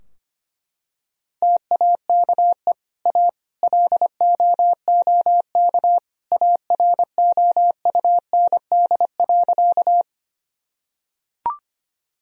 Morse code, heard as a keyed tone, about 25 wpm, 700 Hz. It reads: TAKE A LOOK AROUND.
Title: Sentences from Top 1000 Words - Review Part 3 - 25wpm